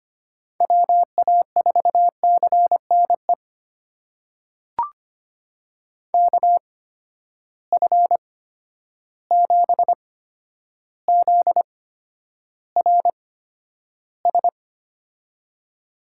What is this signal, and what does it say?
Morse code, heard as a keyed tone, about 25 wpm, spaced as for 11 wpm, 700 Hz. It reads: WA4CNE KF7ZRS
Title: USA Call Signs - 6 Characters - 8x Character Spacing - 25wpm